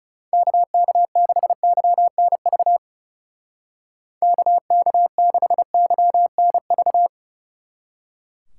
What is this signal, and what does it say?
Morse code, heard as a keyed tone, about 37 wpm, 700 Hz. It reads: KK6YNV KK6YNV
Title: Speed Racing - USA 2x3 Call Signs 30wpm